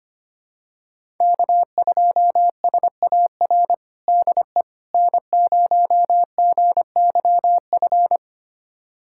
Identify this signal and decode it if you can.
Morse code, heard as a keyed tone, about 25 wpm, 700 Hz. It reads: K2SAR DE N0GYF
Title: USA Callsign DE Callsign - 5 Characters - 25wpm